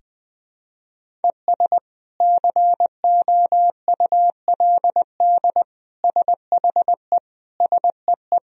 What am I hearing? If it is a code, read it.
Morse code, heard as a keyed tone, about 20 wpm, 700 Hz. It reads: ES COULD SHE SEE
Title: Sentences from Top 100 Words and Abbreviations 20wpm